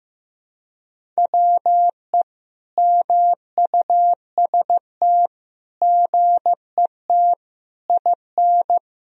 Morse code, heard as a keyed tone, about 15 wpm, 700 Hz. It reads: WE MUST GET IN
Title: Sentences from Top 1000 Words - Review Part 1 - 15wpm